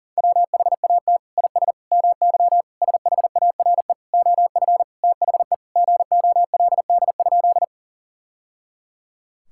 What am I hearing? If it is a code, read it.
Morse code, heard as a keyed tone, about 40 wpm, 700 Hz. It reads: WHAT IS MY SHARE OF THE GOLD?
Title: Sentences from Top 1000 Words - Review Part 6 - 40wpm